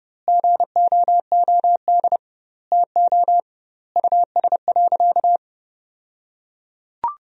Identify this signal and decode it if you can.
Morse code, heard as a keyed tone, about 32 wpm, 700 Hz. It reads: GOOD TO US.
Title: 2x Word-Spacing - Sentences from Top 200 Words - 30wpm